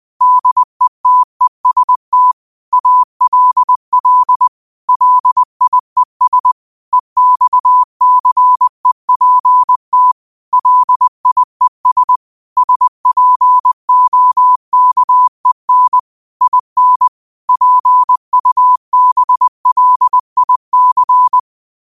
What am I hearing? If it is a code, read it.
Morse code, heard as a keyed tone, about 20 wpm, 1000 Hz. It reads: DETEST ALL LIES EXCEPT LIES SPOKEN IN PUBLIC
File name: qotdfiles / 20wpm